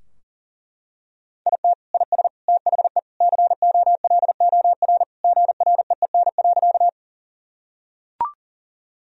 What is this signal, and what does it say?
Morse code, heard as a keyed tone, about 40 wpm, 700 Hz. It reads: IT IS THE COLOR GREEN.
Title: Sentences from Top 1000 Words - Review Part 2 - 40wpm